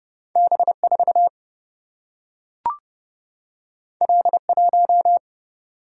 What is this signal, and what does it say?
Morse code, heard as a keyed tone, about 30 wpm, 700 Hz. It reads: B4 L1
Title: Single-Letter Single-Number 30wpm